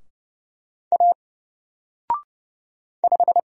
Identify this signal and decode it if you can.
Morse code, heard as a keyed tone, about 30 wpm, 700 Hz. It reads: A 5
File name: Single-Letter-Number-30wpm